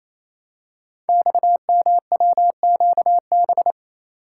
Morse code, heard as a keyed tone, about 27 wpm, 700 Hz. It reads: XMWQB
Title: Code Groups - 5 Letter:Numbers - 28wpm